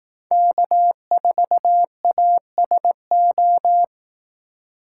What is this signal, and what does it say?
Morse code, heard as a keyed tone, about 18 wpm, 700 Hz. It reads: K4ASO
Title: Speed Racing - USA 1x3 Call Signs 15wpm